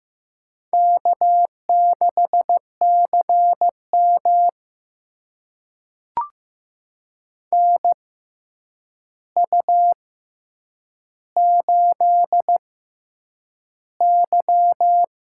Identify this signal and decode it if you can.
Morse code, heard as a keyed tone, about 15 wpm, spaced as for 8 wpm, 700 Hz. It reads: K6CM NU8Y
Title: USA Call Signs - 4 Characters - 6x Character Spacing - 15wpm